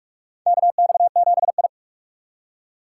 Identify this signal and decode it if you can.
Morse code, heard as a keyed tone, about 46 wpm, 700 Hz. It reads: KX7I
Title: USA Call Signs - 4 Characters - 4x Character Spacing - 45wpm